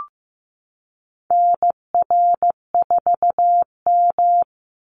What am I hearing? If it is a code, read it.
Morse code, heard as a keyed tone, about 15 wpm, 700 Hz. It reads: NR4M
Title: Call Signs Repeated 2x - 15wpm